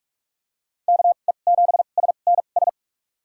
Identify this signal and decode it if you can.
Morse code, heard as a keyed tone, about 46 wpm, spaced as for 32 wpm, 690 Hz. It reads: KE7SNS
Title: USA Call Signs - 6 Characters - 2x Character Spacing - 45wpm